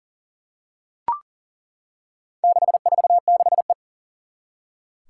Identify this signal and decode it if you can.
Morse code, heard as a keyed tone, about 40 wpm, 700 Hz. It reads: 646E
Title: Four Challenging Letter-Number Combinations 40wpm